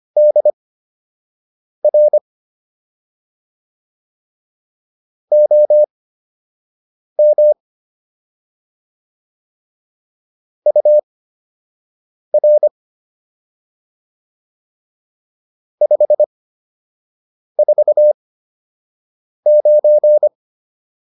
Morse code, heard as a keyed tone, about 25 wpm, spaced as for 6 wpm, 600 Hz. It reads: DR OM UR 549
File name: QSO 204 06